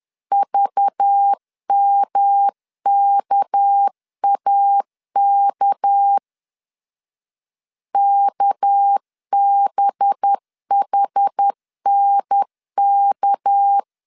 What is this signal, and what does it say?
Morse code, heard as a keyed tone, about 10 wpm, 800 Hz. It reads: VMKAK KBHNK
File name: LEZIONI / 6 PKX0 / 6 PK 10wpm 2436